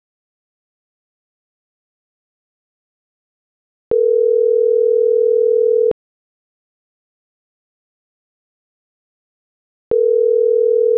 Ringback tone (North America), 440 + 480 Hz.